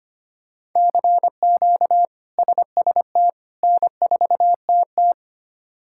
Morse code, heard as a keyed tone, about 25 wpm, 700 Hz. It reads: CQ SST N4TT
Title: K1USN SST QSO Practice - 25wpm